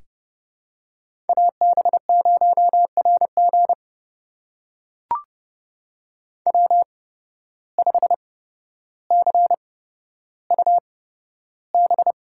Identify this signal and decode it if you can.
Morse code, heard as a keyed tone, about 30 wpm, spaced as for 13 wpm, 700 Hz. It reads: AB0RG W5CUB
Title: USA Call Signs - 5 Characters - 8x Character Spacing - 30wpm